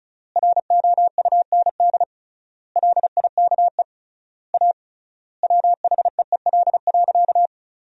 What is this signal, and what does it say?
Morse code, heard as a keyed tone, about 34 wpm, 700 Hz. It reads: ROUND LIKE A WHEEL.
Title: 3x Word-Spacing - Sentences from Top 500 Words - 35wpm